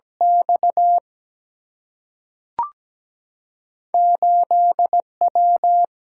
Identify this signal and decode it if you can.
Morse code, heard as a keyed tone, about 17 wpm, 700 Hz. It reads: X 8W
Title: Single-Letter Single-Number 17wpm